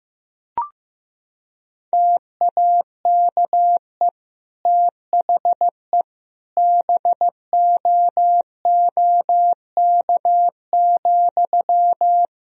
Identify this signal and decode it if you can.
Morse code, heard as a keyed tone, about 16 wpm, 700 Hz. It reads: TAKE THE BOOK,
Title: Sentences from Top 1000 Words - Review Part 1 - 15wpm